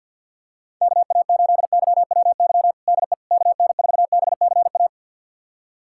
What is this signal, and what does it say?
Morse code, heard as a keyed tone, about 46 wpm, 700 Hz. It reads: KA8YWY DE KN4BCA
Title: USA Callsign DE Callsign - 6 Characters - 50wpm